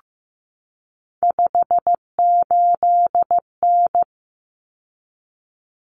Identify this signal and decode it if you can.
Morse code, heard as a keyed tone, about 15 wpm, 700 Hz. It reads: 58N
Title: Common Cut Numbers - 3-Digit 15wpm